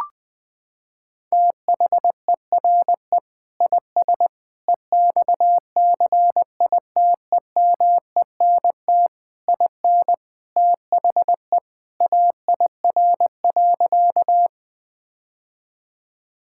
Morse code, heard as a keyed tone, about 20 wpm, 700 Hz. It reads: THERE IS EXCITEMENT IN THE AIR.
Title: Sentences with 3-Word Phrases - 20wpm